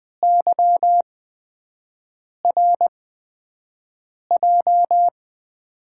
Morse code, heard as a keyed tone, about 20 wpm, spaced as for 5 wpm, 700 Hz. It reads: YRJ
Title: USA Call Signs - 5 Characters - 8x Character Spacing - 20wpm